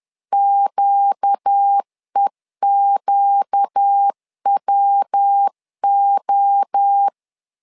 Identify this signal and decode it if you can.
Morse code, heard as a keyed tone, about 10 wpm, 800 Hz. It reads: QEQWO
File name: LEZIONI / 4 Q 10wpm 2210